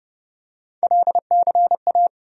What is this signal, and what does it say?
Morse code, heard as a keyed tone, about 30 wpm, 700 Hz. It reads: LCA
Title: USA Call Sign Postfixes - 3 Letters 30wpm